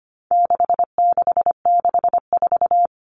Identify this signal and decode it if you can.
Morse code, heard as a keyed tone, about 25 wpm, 700 Hz. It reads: 6664
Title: Numbers - 4-Digit 25wpm